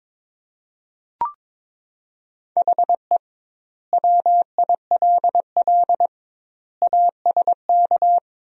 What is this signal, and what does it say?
Morse code, heard as a keyed tone, about 22 wpm, 700 Hz. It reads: HE WILL ASK
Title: 2x Word-Spacing - Sentences from Top 200 Words - 22wpm